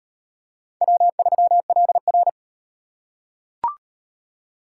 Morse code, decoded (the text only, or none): W3LR